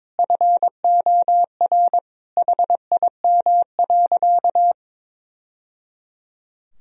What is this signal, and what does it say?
Morse code, heard as a keyed tone, about 22 wpm, 700 Hz. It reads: FOR HIM.
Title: Sentences from Top 1000 Words - Review Part 1 - 22wpm